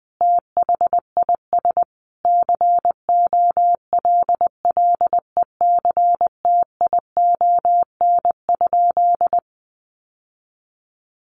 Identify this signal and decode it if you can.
Morse code, heard as a keyed tone, about 20 wpm, 700 Hz. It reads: THIS COLLECTION?
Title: Sentences from Top 1200 Words 20wpm